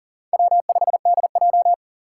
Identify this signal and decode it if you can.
Morse code, heard as a keyed tone, about 40 wpm, 700 Hz. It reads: W5DJ
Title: Four Challenging Letter-Number Combinations 40wpm